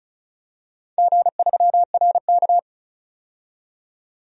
Morse code, heard as a keyed tone, about 36 wpm, 700 Hz. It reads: G3RK
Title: DXCC-100 Call Signs - 4 Characters - 35wpm